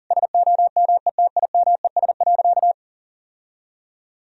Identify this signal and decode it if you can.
Morse code, heard as a keyed tone, about 37 wpm, 700 Hz. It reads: SOMETIMES.
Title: Sentences from Top 1000 Words - Review Part 3 - 40wpm